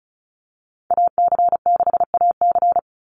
Morse code, heard as a keyed tone, about 34 wpm, 700 Hz. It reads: AC6AC
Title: Call Signs - 35wpm